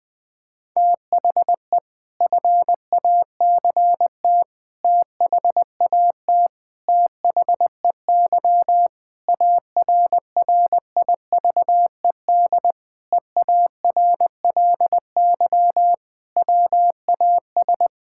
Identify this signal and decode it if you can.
Morse code, heard as a keyed tone, about 21 wpm, 700 Hz. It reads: THE FACT THAT THEY ARRIVED EARLY WAS
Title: Sentences with 4-Word Phrases - Spoken Beginning - 20wpm